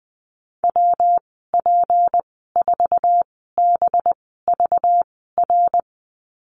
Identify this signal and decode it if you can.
Morse code, heard as a keyed tone, about 20 wpm, spaced as for 14 wpm, 700 Hz. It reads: WP4BVR
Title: 238 Lesson 38g - Letter q - Call Signs 20wpm